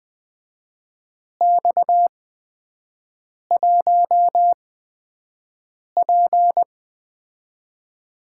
Morse code, heard as a keyed tone, about 20 wpm, spaced as for 5 wpm, 700 Hz. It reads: X1P
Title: USA Call Signs - 4 Characters - 8x Character Spacing - 20wpm